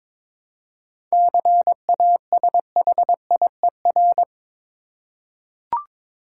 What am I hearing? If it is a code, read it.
Morse code, heard as a keyed tone, about 22 wpm, 700 Hz. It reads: CASHIER
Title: Practice QSO - Exchange 3 - More Info - 22wpm